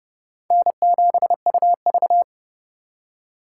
Morse code, heard as a keyed tone, about 40 wpm, 700 Hz. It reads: N7UV